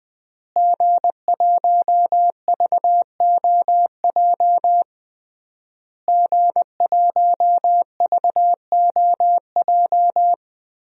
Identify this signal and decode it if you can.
Morse code, heard as a keyed tone, about 21 wpm, 700 Hz. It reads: G1VOJ G1VOJ
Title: DXCC-100 Call Signs - 5 Characters - Repeated 3x - 20wpm